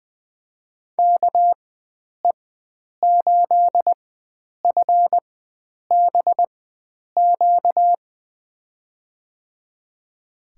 Morse code, heard as a keyed tone, about 20 wpm, spaced as for 9 wpm, 700 Hz. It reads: KE8FBQ
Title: USA Call Signs - 6 Characters - 4x Character Spacing - 20wpm